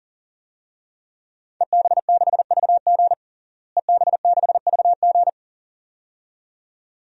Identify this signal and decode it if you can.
Morse code, heard as a keyed tone, about 37 wpm, 700 Hz. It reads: EB6VG EB6VG
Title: DXCC-100 Call Signs - 5 Characters - Repeated 2x - 40wpm